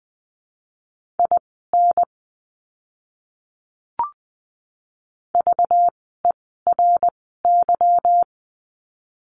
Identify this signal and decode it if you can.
Morse code, heard as a keyed tone, about 20 wpm, spaced as for 15 wpm, 700 Hz. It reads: IN VERY
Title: 192 Lesson 31f - Letter v - Words 20wpm